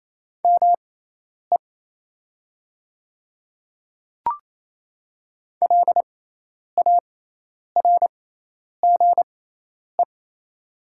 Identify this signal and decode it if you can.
Morse code, heard as a keyed tone, about 28 wpm, spaced as for 10 wpm, 700 Hz. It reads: ME LARGE
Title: 6x Character Spacing - Top 500 Words - 28wpm